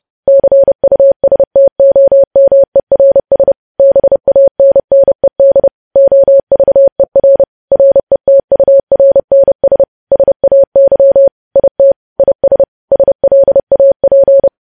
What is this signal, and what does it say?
Morse code, heard as a keyed tone, about 30 wpm, 550 Hz. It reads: CUSTOMERS BANNED OVER RETURNS SAY IT IS SLAP
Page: News Headlines in Morse Code at 30 WPM